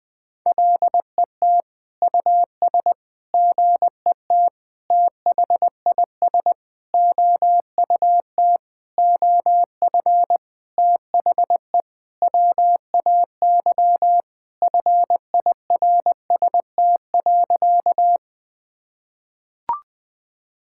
Morse code, heard as a keyed tone, about 21 wpm, 700 Hz. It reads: LET US GET THIS OUT OF THE WAY FIRST.
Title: Sentences with 7-Word Phrases - 20wpm